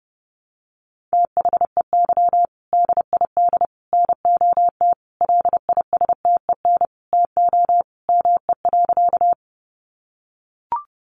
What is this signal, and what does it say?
Morse code, heard as a keyed tone, about 30 wpm, 700 Hz. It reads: THEY DID NOT LISTEN TO ME.
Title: Sentences from Top 600 Words 30wpm